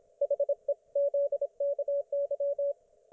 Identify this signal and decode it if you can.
Morse code, heard as a keyed tone, about 27 wpm, 570 Hz. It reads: HEZKY